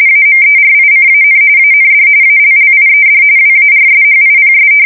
RTTY, about 45.45 baud. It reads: AMERICAN AND ALLIED WARSHIPS
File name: fsk170 / battleshipnj